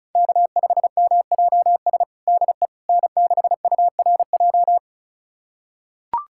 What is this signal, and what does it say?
Morse code, heard as a keyed tone, about 36 wpm, 690 Hz. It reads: K5MJS DE N6URJ
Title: USA Callsign DE Callsign - 5 Characters - 35wpm